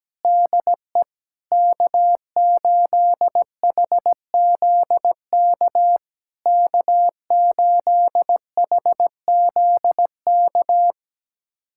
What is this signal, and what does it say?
Morse code, heard as a keyed tone, about 17 wpm, 700 Hz. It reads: DE K8HZK K8HZK
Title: SOTA QSO Practice 17wpm